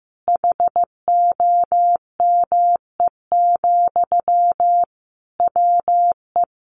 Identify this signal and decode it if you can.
Morse code, heard as a keyed tone, about 15 wpm, 700 Hz. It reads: HOME, WE
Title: Sentences from Top 1400 Words 15wpm